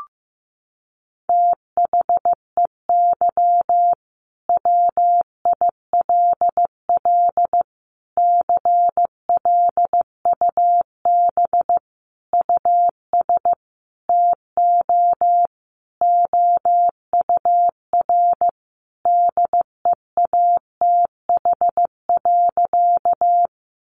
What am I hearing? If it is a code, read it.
Morse code, heard as a keyed tone, about 15 wpm, 700 Hz. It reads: THEY WILL CLUB US TO OUR DEATH.
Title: Sentences from Top 1200 Words 15wpm